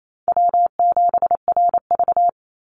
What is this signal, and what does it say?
Morse code, heard as a keyed tone, about 28 wpm, 700 Hz. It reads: W7RV
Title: NA SOTA Chaser Callsigns 28wpm